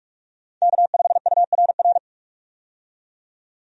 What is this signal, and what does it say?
Morse code, heard as a keyed tone, about 45 wpm, 700 Hz. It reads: K5URR
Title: USA Call Signs - 5 Characters - 45wpm